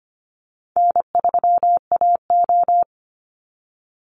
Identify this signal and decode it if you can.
Morse code, heard as a keyed tone, about 25 wpm, 700 Hz. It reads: N3AO
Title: Easy Call Signs - 25wpm